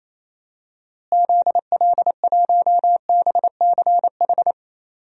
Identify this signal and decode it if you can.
Morse code, heard as a keyed tone, about 28 wpm, 700 Hz. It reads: ZL1BCH
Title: DXCC-100 Call Signs - 6 Characters - Repeated 2x - 28wpm